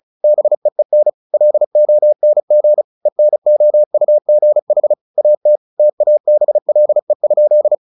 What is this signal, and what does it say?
Morse code, heard as a keyed tone, about 34 wpm, 600 Hz. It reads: BEEN LONG ENOUGH AT TABLE?
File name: the old man and the sea / the old man and the sea 1 35wpm